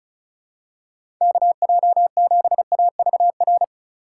Morse code, heard as a keyed tone, about 34 wpm, 700 Hz. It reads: KJ7AVR
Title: USA Call Signs - 6 Characters - 2x Character Spacing - 35wpm